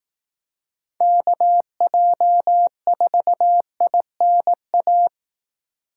Morse code, heard as a keyed tone, about 18 wpm, 700 Hz. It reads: KJ4INA
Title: Speed Racing - USA 2x3 Call Signs 15wpm